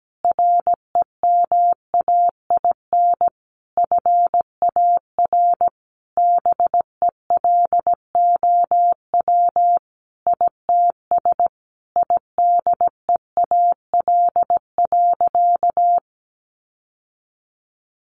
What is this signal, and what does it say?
Morse code, heard as a keyed tone, about 17 wpm, 700 Hz. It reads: REMAIN FAR BELOW ITS IDEAL.
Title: Sentences from Top 1400 Words 17wpm